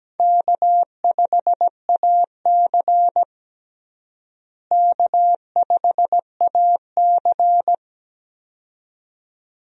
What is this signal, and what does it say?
Morse code, heard as a keyed tone, about 17 wpm, 700 Hz. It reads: K5AC K5AC
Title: USA Call Signs - 4 Characters - Repeated 2x - 17wpm